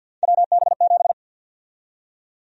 Morse code, heard as a keyed tone, about 53 wpm, 700 Hz. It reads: WB7
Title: USA Call Sign Prefixes - 50wpm